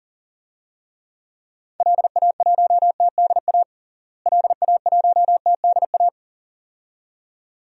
Morse code, heard as a keyed tone, about 40 wpm, 700 Hz. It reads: LA1TDA LA1TDA
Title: DXCC-100 Call Signs - 6 Characters - Repeated 2x - 40wpm